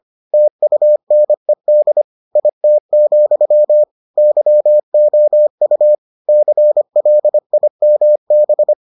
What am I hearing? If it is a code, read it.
Morse code, heard as a keyed tone, about 25 wpm, 600 Hz. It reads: TUNED IT, YOU CLIMB